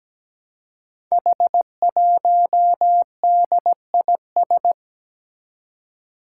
Morse code, heard as a keyed tone, about 17 wpm, 700 Hz. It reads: H1DIS
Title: Five Challenging Letter-Number Combinations 17wpm